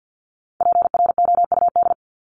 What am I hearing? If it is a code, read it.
Morse code, heard as a keyed tone, about 50 wpm, 700 Hz. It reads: PROUD
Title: Top 1501-1750 Words 50wpm